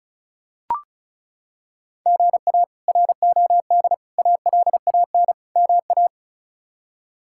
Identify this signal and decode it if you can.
Morse code, heard as a keyed tone, about 36 wpm, 700 Hz. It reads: GA ROD ALAN MA
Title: K1USN SST QSO Practice - 35wpm